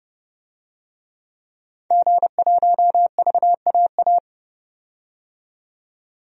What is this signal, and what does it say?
Morse code, heard as a keyed tone, about 30 wpm, 700 Hz. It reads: G1VAA
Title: DXCC-100 Call Signs - 5 Characters - 30wpm